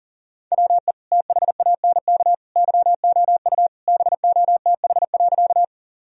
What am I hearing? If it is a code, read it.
Morse code, heard as a keyed tone, about 40 wpm, 700 Hz. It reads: WE THANK YOU BOTH.
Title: Sentences from Top 1000 Words - Review Part 1 - 40wpm